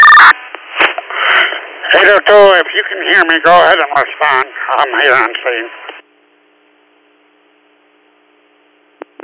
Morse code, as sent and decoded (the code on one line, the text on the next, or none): none